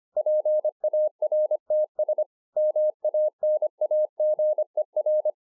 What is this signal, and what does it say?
Morse code, heard as a keyed tone, about 26 wpm, 610 Hz. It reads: PARTS MANAGER
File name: 2011-04-mp3 / THE 3.11 3 25WPM